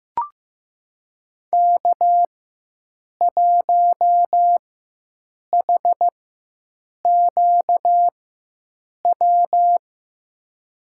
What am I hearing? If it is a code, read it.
Morse code, heard as a keyed tone, about 15 wpm, spaced as for 7 wpm, 700 Hz. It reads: K1HQW
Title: USA Call Signs - 5 Characters - 4x Character Spacing - 15wpm